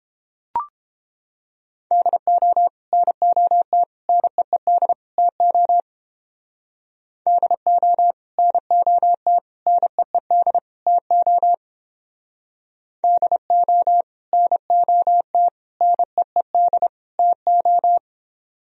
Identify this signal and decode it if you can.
Morse code, heard as a keyed tone, about 29 wpm, 700 Hz. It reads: DO NOT NEED TO DO NOT NEED TO DO NOT NEED TO